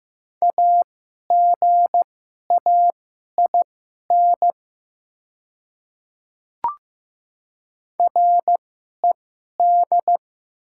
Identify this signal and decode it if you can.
Morse code, heard as a keyed tone, about 15 wpm, spaced as for 11 wpm, 700 Hz. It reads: AGAIN RED
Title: 2x Character Spacing - Top 500 Words - 15wpm